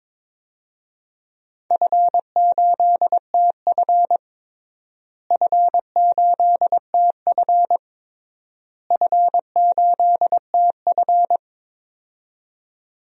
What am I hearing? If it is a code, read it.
Morse code, heard as a keyed tone, about 22 wpm, 700 Hz. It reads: F8TF F8TF F8TF
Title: DXCC-100 Call Signs - 4 Characters - Repeated 3x - 22wpm